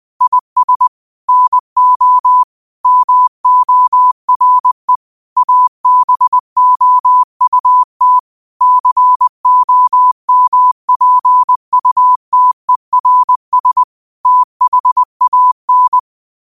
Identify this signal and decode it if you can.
Morse code, heard as a keyed tone, about 20 wpm, 1000 Hz. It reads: IS NO MORE ABOUT COMPUTERS THAN